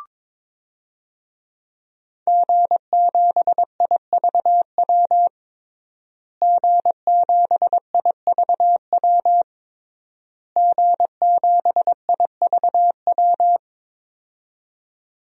Morse code, heard as a keyed tone, about 22 wpm, 700 Hz. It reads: G7IVW G7IVW G7IVW
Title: DXCC-100 Call Signs - 5 Characters - Repeated 3x - 22wpm